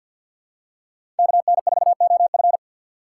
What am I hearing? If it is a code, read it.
Morse code, heard as a keyed tone, about 51 wpm, 690 Hz. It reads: KN4OF
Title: USA Call Signs - 5 Characters - 3x Character Spacing - 50wpm